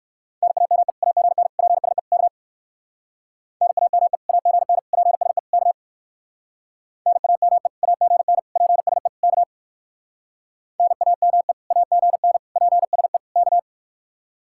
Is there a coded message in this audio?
Yes: Morse code from 0.4 to 13.6 s, heard as a keyed tone, about 58 wpm, 700 Hz. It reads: NAME AGN PSE K NAME AGN PSE K NAME AGN PSE K NAME AGN PSE K